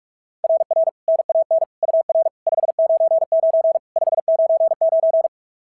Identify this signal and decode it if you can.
Morse code, heard as a keyed tone, about 44 wpm, 660 Hz. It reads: RR NAN UR 599 599
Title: SOTA QSO Practice 45wpm